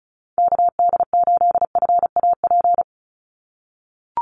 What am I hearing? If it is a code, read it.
Morse code, heard as a keyed tone, about 34 wpm, 700 Hz. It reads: KD8FAP
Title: USA Call Signs 35wpm